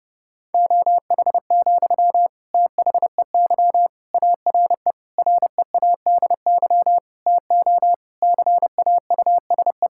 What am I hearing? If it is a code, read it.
Morse code, heard as a keyed tone, about 30 wpm, 700 Hz. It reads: OH, THEY ARE READY TO CAUSE
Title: Sentences from Top 1600 Words 30wpm